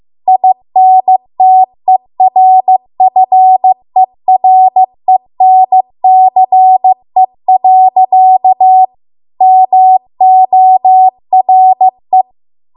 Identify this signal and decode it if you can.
Morse code, heard as a keyed tone, about 15 wpm, 750 Hz. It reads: INTERFERENCE. MORE